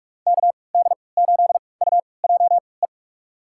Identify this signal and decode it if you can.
Morse code, heard as a keyed tone, about 42 wpm, spaced as for 25 wpm, 700 Hz. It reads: KD8UJE
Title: USA Call Signs - 6 Characters - 3x Character Spacing - 45wpm